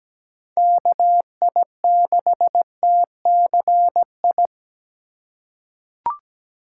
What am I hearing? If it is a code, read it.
Morse code, heard as a keyed tone, about 17 wpm, 700 Hz. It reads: KI6TCI
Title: USA Call Signs - 6 Characters - 3x Character Spacing - 17wpm